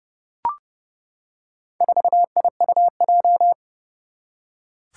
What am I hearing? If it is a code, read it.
Morse code, heard as a keyed tone, about 30 wpm, 700 Hz. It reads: 4IUJ
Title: Four Challenging Letter-Number Combinations 30wpm